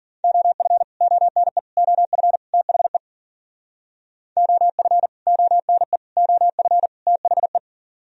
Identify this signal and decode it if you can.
Morse code, heard as a keyed tone, about 51 wpm, 700 Hz. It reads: OF ONE OF THE OF ONE OF THE